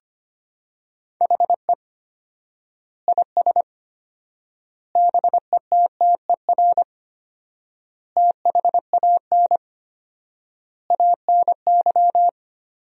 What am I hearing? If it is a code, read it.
Morse code, heard as a keyed tone, about 25 wpm, 700 Hz. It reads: HE IS BETTER THAN ANY
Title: 4x Word-Spacing - Sentences from Top 400 Words - 25wpm